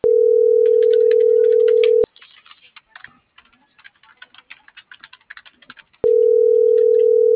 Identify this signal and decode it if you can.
Ringback tone (North America), 440 + 480 Hz.